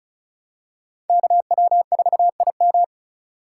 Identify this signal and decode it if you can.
Morse code, heard as a keyed tone, about 34 wpm, 700 Hz. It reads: KW4IM
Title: USA Call Signs - 5 Characters - 3x Character Spacing - 35wpm